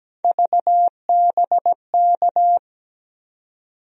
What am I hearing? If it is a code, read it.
Morse code, heard as a keyed tone, about 17 wpm, 700 Hz. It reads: VBK
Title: USA Call Sign Postfixes - 3 Letters 17wpm